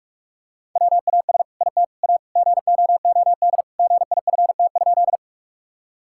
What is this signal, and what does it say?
Morse code, heard as a keyed tone, about 46 wpm, 700 Hz. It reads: WAS IT A GOOD GIFT?
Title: Sentences from Top 1000 Words - Review Part 6 - 45wpm